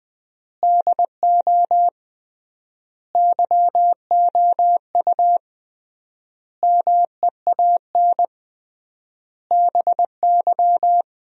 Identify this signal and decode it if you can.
Morse code, heard as a keyed tone, about 20 wpm, 700 Hz. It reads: DO YOU MEAN BY
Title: 3x Word-Spacing - Sentences from Top 200 Words - 20wpm